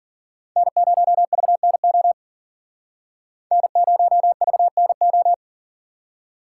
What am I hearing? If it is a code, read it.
Morse code, heard as a keyed tone, about 51 wpm, 700 Hz. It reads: N0VNO N0VNO